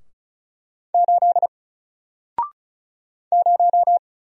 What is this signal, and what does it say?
Morse code, heard as a keyed tone, about 35 wpm, 700 Hz. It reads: 8 0
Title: Single Character (Letter:Number:Punctuation:Prosign) 35wpm